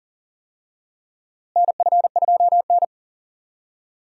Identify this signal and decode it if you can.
Morse code, heard as a keyed tone, about 40 wpm, 700 Hz. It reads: NF2N
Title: USA Call Signs - 4 Characters - Repeated 3x - 40wpm